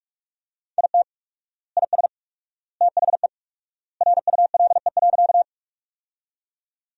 Morse code, heard as a keyed tone, about 46 wpm, 700 Hz. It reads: IT IS THE RULE.
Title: 4x Word-Spacing - Sentences from Top 400 Words - 45wpm